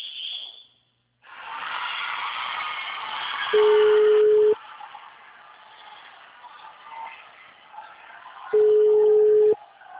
Ringback tone (Europe, 425 Hz).